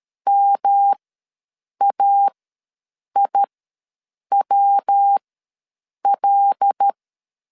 Morse code, heard as a keyed tone, about 13 wpm, spaced as for 7 wpm, 800 Hz. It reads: MAIWL
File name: LEZIONI / 5 LBVC / 5 L ws 12wpm 2958